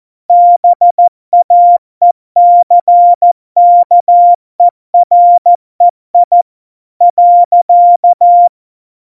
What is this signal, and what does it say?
Morse code, heard as a keyed tone, about 14 wpm, 700 Hz. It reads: BAECKEREI .
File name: Morsekurs / Witze / Witz 03 April 2018, 70 F80